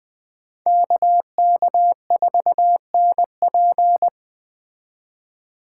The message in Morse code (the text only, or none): KK4NP